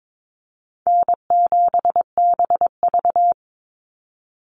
Morse code, heard as a keyed tone, about 22 wpm, 700 Hz. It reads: N7BV
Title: Call Signs - 22wpm